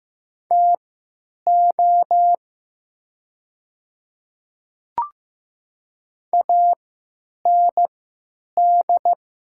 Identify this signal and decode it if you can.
Morse code, heard as a keyed tone, about 15 wpm, spaced as for 9 wpm, 700 Hz. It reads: TO AND
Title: 3x Character Spacing - Top 200 Words - 15wpm